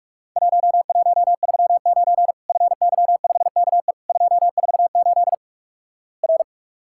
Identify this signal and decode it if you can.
Morse code, heard as a keyed tone, about 44 wpm, 700 Hz. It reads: 1139 FY5KE 248 R
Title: Worked All Europe DX Contest - QTC Exchange - 45wpm